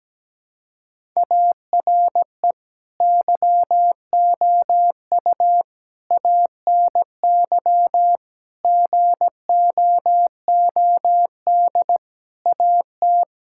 Morse code, heard as a keyed tone, about 17 wpm, 700 Hz. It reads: ARE YOU ANY GOOD AT
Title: Sentences from Top 1000 Words - Review Part 6 - 17wpm